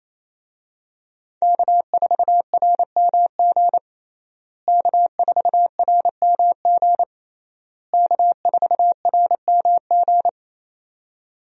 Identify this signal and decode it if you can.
Morse code, heard as a keyed tone, about 28 wpm, 700 Hz. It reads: K4RMG K4RMG K4RMG
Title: USA Call Signs - 5 Characters - Repeated 3x - 28wpm